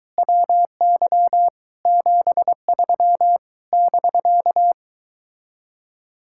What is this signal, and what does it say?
Morse code, heard as a keyed tone, about 24 wpm, 700 Hz. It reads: WY 73 <BK>